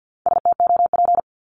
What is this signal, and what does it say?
Morse code, heard as a keyed tone, about 50 wpm, 700 Hz. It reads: STOP
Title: Multi-letter Phonemes 50wpm